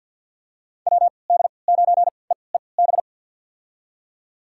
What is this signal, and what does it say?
Morse code, heard as a keyed tone, about 48 wpm, spaced as for 28 wpm, 700 Hz. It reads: WD9EEB
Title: USA Call Signs - 6 Characters - 3x Character Spacing - 50wpm